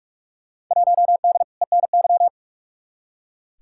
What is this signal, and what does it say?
Morse code, heard as a keyed tone, about 44 wpm, 700 Hz. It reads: 1D ENY
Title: Field Day Exchanges Encore - 45wpm